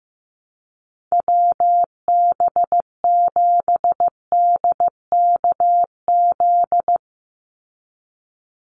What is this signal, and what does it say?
Morse code, heard as a keyed tone, about 15 wpm, 700 Hz. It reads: WB7DKZ
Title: Call Signs Repeated 3x - 15wpm